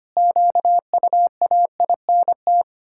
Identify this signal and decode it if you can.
Morse code, heard as a keyed tone, about 25 wpm, 700 Hz. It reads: QUAINT
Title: Pangram Challenge v2 - 25wpm